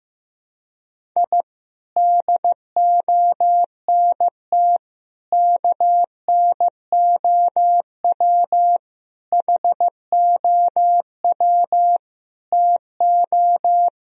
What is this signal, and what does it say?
Morse code, heard as a keyed tone, about 15 wpm, 700 Hz. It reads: I DONT KNOW HOW TO
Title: Sentences from Top 1000 Words - Review Part 1 - 15wpm